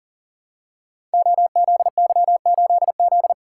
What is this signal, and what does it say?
Morse code, heard as a keyed tone, about 40 wpm, 700 Hz. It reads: OZY8Z
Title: Code Groups - 5 Letter:Numbers - 40wpm